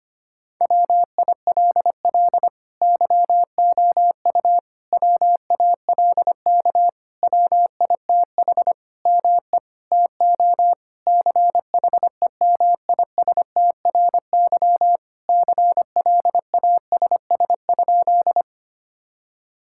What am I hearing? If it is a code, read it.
Morse code, heard as a keyed tone, about 25 wpm, 700 Hz. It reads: WILL YOU WALK WITH ME TO CHEMISTRY CLASS?
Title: Sentences from Top 1000 Words - Review Part 6 - 25wpm